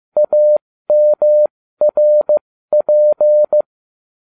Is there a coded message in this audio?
Yes: Morse code from 0.2 to 3.6 s, heard as a keyed tone, about 15 wpm, spaced as for 13 wpm, 600 Hz. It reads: AMRP